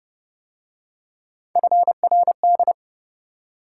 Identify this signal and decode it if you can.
Morse code, heard as a keyed tone, about 29 wpm, 700 Hz. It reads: FRD
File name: Speed-Racing - Three Letter-Number Combinations - 25wpm